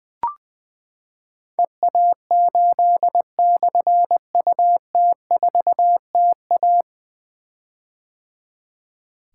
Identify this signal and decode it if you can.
Morse code, heard as a keyed tone, about 20 wpm, 700 Hz. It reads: EA8/UT4TA
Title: Difficult Call Signs 20wpm